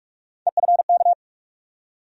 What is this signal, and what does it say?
Morse code, heard as a keyed tone, about 47 wpm, 700 Hz. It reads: EFK
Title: USA Call Sign Postfixes - 3 Letters 45wpm